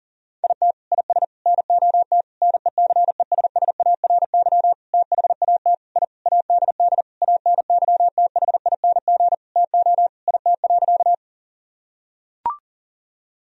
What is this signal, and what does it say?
Morse code, heard as a keyed tone, about 40 wpm, 700 Hz. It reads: IT IS NOT NECESSARY THAT I ADD ANYTHING TO IT.
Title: Sentences from Top 1000 Words - Review Part 1 - 40wpm